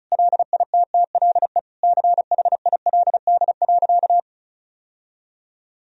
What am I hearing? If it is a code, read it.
Morse code, heard as a keyed tone, about 36 wpm, 700 Hz. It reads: LITTLE CHILD.
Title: Sentences from Top 1000 Words - Review Part 1 - 35wpm